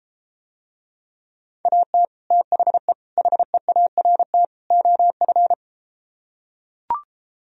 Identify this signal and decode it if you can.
Morse code, heard as a keyed tone, about 32 wpm, 700 Hz. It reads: AT THE HEART OF